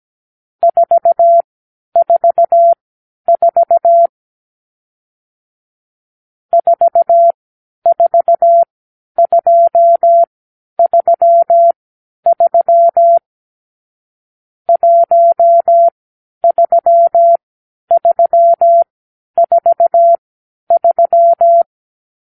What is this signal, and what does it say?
Morse code, heard as a keyed tone, about 17 wpm, spaced as for 10 wpm, 680 Hz. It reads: 444 44233 13343
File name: k7qo / Lesson 46 (1-4)